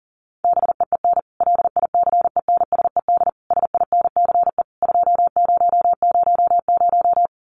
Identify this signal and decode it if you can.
Morse code, heard as a keyed tone, about 40 wpm, 700 Hz. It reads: BEEN LICENSED SINCE 2000
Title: Common QSO Phrases 40wpm